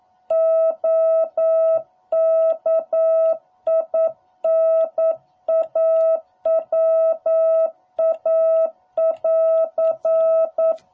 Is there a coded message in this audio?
Yes: Morse code from 0.3 to 10.7 s, heard as a keyed tone, about 9 wpm, 660 Hz. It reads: OKINAWA <AR>